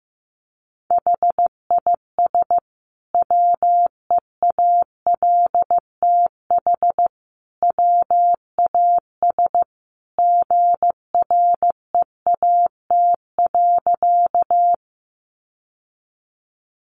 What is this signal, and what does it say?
Morse code, heard as a keyed tone, about 15 wpm, 700 Hz. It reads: HIS WEALTH WAS GREAT.
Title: Sentences from Top 1200 Words 15wpm